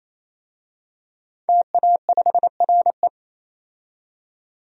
Morse code, heard as a keyed tone, about 28 wpm, 700 Hz. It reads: TA5RE
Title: DXCC-100 Call Signs - 5 Characters - 28wpm